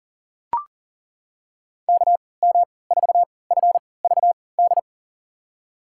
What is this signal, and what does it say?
Morse code, heard as a keyed tone, about 40 wpm, spaced as for 23 wpm, 700 Hz. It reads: KM4FVD
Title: USA Call Signs - 6 Characters - 3x Character Spacing - 40wpm